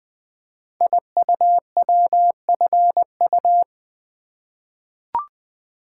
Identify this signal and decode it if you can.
Morse code, heard as a keyed tone, about 20 wpm, 700 Hz. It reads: IUWFU
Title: Code Groups - 5 Letters - 20wpm